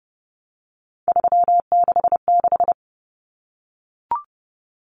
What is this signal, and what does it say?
Morse code, heard as a keyed tone, about 30 wpm, 700 Hz. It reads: 366
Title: Numbers - 3-Digit 30wpm